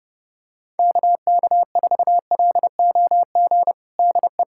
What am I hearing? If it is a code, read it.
Morse code, heard as a keyed tone, about 30 wpm, 700 Hz. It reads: KK4LOG DE
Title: USA Callsign DE Callsign - 6 Characters - 30wpm